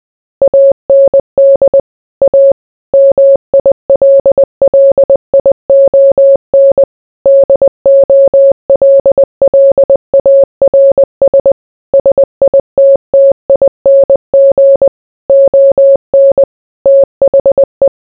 Morse code, heard as a keyed tone, about 20 wpm, 550 Hz. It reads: AND A MILLION DOLLARS SITTING ON THE